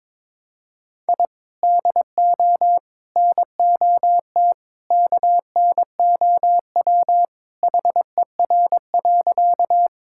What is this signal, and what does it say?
Morse code, heard as a keyed tone, about 22 wpm, 700 Hz. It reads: I DO NOT KNOW HER.
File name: practice / 22wpm / Sentences with 4-Word Phrases - 22wpm